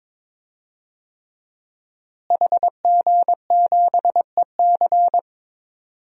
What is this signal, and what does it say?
Morse code, heard as a keyed tone, about 22 wpm, 700 Hz. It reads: HG7EC
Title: DXCC-100 Call Signs - 5 Characters - Repeated 3x - 22wpm